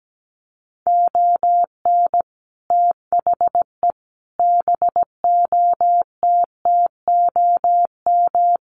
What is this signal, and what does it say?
Morse code, heard as a keyed tone, about 17 wpm, 700 Hz. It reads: ON THE BOTTOM
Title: Sets of 5 Words Encore2 17wpm